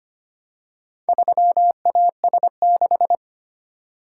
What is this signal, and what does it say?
Morse code, heard as a keyed tone, about 25 wpm, 700 Hz. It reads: 3AS6
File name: practice / 25wpm / Four Letter-Number Combinations - 25wpm